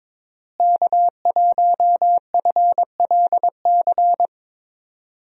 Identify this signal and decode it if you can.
Morse code, heard as a keyed tone, about 22 wpm, 700 Hz. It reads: K1FLC
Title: USA Call Signs - 5 Characters - Repeated 2x - 22wpm